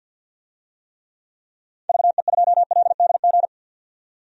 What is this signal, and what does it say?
Morse code, heard as a keyed tone, about 51 wpm, 700 Hz. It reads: VE2LDG
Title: DXCC-100 Call Signs - 6 Characters - 50wpm